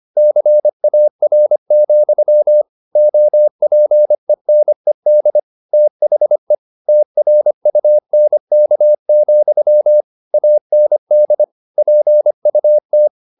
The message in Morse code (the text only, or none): CAR, OPENED THE TRUNK, AND PUT